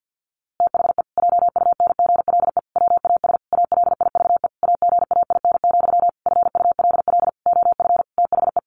Morse code, heard as a keyed tone, about 50 wpm, 700 Hz. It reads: THE JUNGLE WAS ALIVE AGAIN, FULL OF THE
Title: A Sound Of Thunder 50wpm